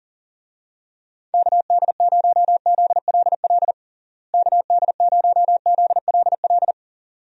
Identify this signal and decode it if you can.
Morse code, heard as a keyed tone, about 40 wpm, 700 Hz. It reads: KD0ZLL KD0ZLL
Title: USA Call Signs - 6 Characters - Repeated 2x - 40wpm